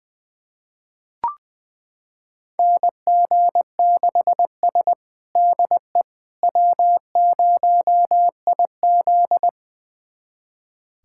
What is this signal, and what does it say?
Morse code, heard as a keyed tone, about 21 wpm, 700 Hz. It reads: NG6S DE W0IZ
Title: USA Callsign DE Callsign - 4 Characters - 20wpm